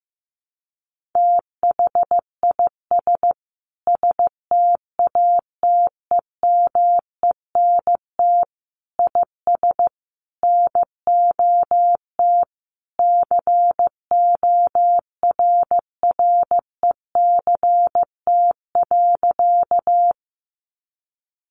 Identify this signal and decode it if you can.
Morse code, heard as a keyed tone, about 14 wpm, 700 Hz. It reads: THIS STATEMENT IS NOT CORRECT.
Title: Sentences from Top 1200 Words 15wpm